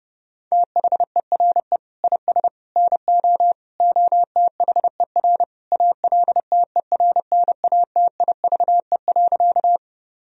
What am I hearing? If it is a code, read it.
Morse code, heard as a keyed tone, about 30 wpm, 700 Hz. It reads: THERE IS NO OTHER ALTERNATIVE.
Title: Sentences from Top 1000 Words - Review Part 4 - 30wpm